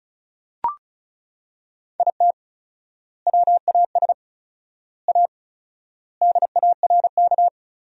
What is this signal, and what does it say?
Morse code, heard as a keyed tone, about 36 wpm, 700 Hz. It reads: IT WAS A DARK
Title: 4x Word-Spacing - Sentences from Top 500 Words - 35wpm